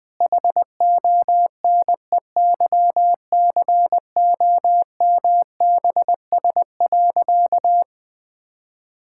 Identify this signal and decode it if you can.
Morse code, heard as a keyed tone, about 21 wpm, 700 Hz. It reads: HONEYCOMBS.
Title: All-Age Appropriate Jokes - 20wpm